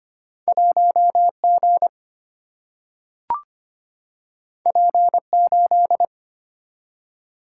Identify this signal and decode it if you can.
Morse code, heard as a keyed tone, about 25 wpm, 700 Hz. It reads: 1 G P 8
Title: Single-Letter Single-Number 25wpm